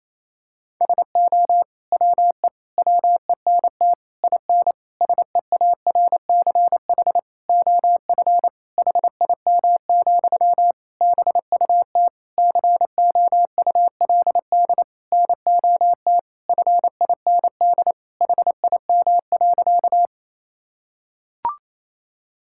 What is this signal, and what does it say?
Morse code, heard as a keyed tone, about 28 wpm, 700 Hz. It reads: SO WE WENT IN SEARCH OF HIM, BUT COULD NOT FIND HIM.
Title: Sentences from Top 1000 Words - Review Part 5 - 28wpm